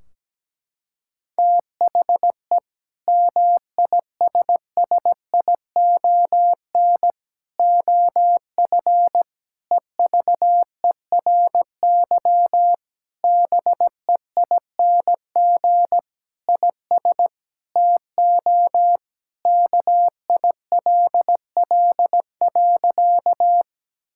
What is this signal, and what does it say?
Morse code, heard as a keyed tone, about 17 wpm, 700 Hz. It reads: THE MISSION OF EVERY BEING IS TO KILL.
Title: Sentences from Top 1600 Words 17wpm